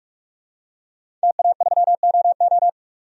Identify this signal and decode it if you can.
Morse code, heard as a keyed tone, about 46 wpm, 700 Hz. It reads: TA3OO
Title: DXCC-100 Call Signs - 5 Characters - 45wpm